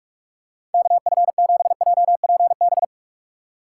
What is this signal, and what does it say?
Morse code, heard as a keyed tone, about 44 wpm, 700 Hz. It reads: KF7JPB
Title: USA Call Signs - 6 Characters - 4x Character Spacing - 45wpm